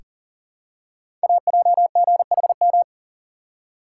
Morse code, heard as a keyed tone, about 39 wpm, 690 Hz. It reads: AJGHM
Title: Code Groups - 5 Letters - 40wpm